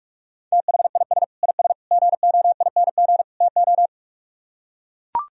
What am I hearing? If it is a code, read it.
Morse code, heard as a keyed tone, about 44 wpm, 700 Hz. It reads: THIS IS GOING TO